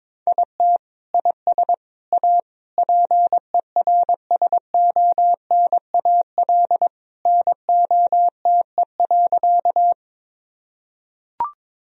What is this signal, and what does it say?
Morse code, heard as a keyed tone, about 22 wpm, 700 Hz. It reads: IT IS A PERSONAL NOTE.
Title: Sentences from Top 1000 Words - Review Part 4 - 22wpm